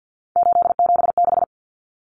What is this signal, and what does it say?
Morse code, heard as a keyed tone, about 50 wpm, 700 Hz. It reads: 876
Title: Numbers - 3-Digit 50wpm